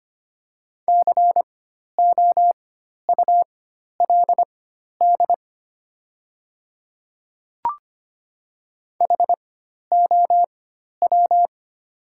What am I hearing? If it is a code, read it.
Morse code, heard as a keyed tone, about 25 wpm, spaced as for 12 wpm, 700 Hz. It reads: COULD HOW
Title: 4x Character Spacing - Top 200 Words - 25wpm